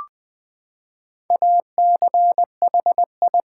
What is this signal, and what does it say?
Morse code, heard as a keyed tone, about 20 wpm, 700 Hz. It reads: ACHI
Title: Build A Word - 4 Letters At A Time - 11-Letter Words - 20wpm